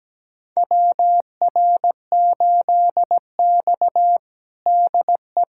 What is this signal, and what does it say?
Morse code, heard as a keyed tone, about 17 wpm, 700 Hz. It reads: WR8X DE
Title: USA Callsign DE Callsign - 4 Characters - 17wpm